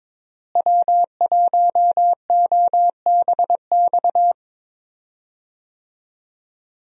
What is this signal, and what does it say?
Morse code, heard as a keyed tone, about 22 wpm, 700 Hz. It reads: W1OBX
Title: USA Call Signs - 5 Characters - Repeated 3x - 22wpm